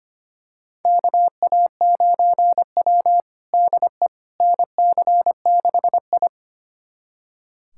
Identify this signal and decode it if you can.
Morse code, heard as a keyed tone, about 25 wpm, 700 Hz. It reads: KA9W DE NC6I
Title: USA Callsign DE Callsign - 4 Characters - 25wpm